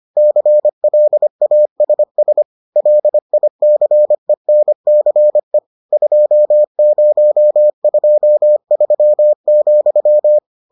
Morse code, heard as a keyed tone, about 25 wpm, 600 Hz. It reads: CLASS LICENCE 2023,